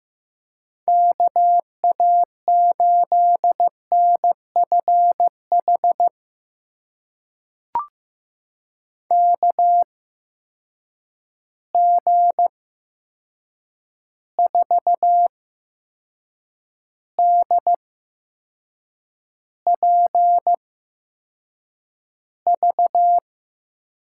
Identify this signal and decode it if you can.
Morse code, heard as a keyed tone, about 15 wpm, spaced as for 6 wpm, 700 Hz. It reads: KA8NFH KG4DPV
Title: USA Call Signs - 6 Characters - 8x Character Spacing - 15wpm